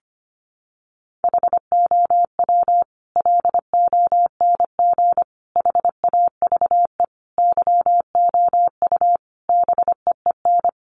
Morse code, heard as a keyed tone, about 25 wpm, 700 Hz. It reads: HOW LONG HAVE YOU BEEN
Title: Sentences from Top 200 Words 25wpm